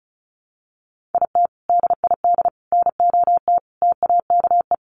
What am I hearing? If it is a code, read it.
Morse code, heard as a keyed tone, about 34 wpm, 700 Hz. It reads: IT DID NOT TAKE
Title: Sets of 5 Words Encore2 35wpm